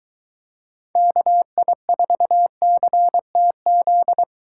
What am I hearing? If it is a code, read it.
Morse code, heard as a keyed tone, about 23 wpm, 700 Hz. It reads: KI4CTZ